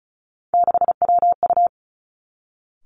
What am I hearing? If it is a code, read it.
Morse code, heard as a keyed tone, about 36 wpm, 700 Hz. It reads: 6WU
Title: Three Challenging Letter-Number Combinations 35wpm